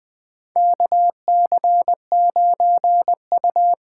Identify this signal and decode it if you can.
Morse code, heard as a keyed tone, about 20 wpm, 700 Hz. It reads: KC9U
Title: USA Call Signs - 6 Characters - 8x Character Spacing - 20wpm